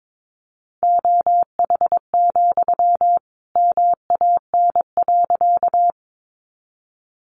Morse code, heard as a keyed tone, about 22 wpm, 700 Hz. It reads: OH, MAN.
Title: Sentences from Top 500 Words 22wpm